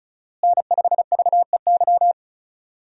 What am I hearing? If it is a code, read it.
Morse code, heard as a keyed tone, about 34 wpm, 700 Hz. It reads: N5VEY
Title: USA Call Signs - 5 Characters - Repeated 3x - 35wpm